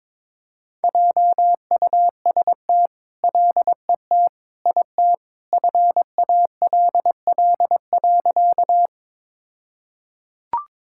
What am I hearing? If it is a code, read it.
Morse code, heard as a keyed tone, about 22 wpm, 700 Hz. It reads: JUST LET IT FALL.
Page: morse-code-ninja-practice-22wpm